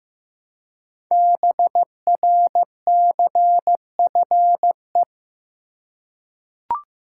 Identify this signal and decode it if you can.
Morse code, heard as a keyed tone, about 15 wpm, 700 Hz. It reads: BRCFE
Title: Code Groups - 5 Letters - 15wpm